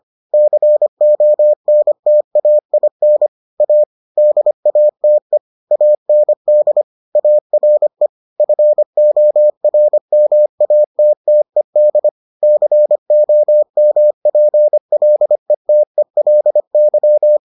Morse code, heard as a keyed tone, about 25 wpm, 600 Hz. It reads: CONTAIN A DATE AND ARE FORMATTED COMPLETELY